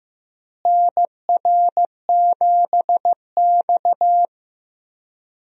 Morse code, heard as a keyed tone, about 15 wpm, 700 Hz. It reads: NR7X
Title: USA Call Signs - 4 Characters - Repeated 3x - 15wpm